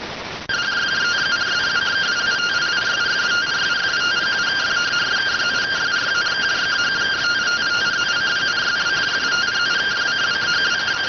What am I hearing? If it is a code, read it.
Two-tone teleprinter signal, about 170 Hz shift.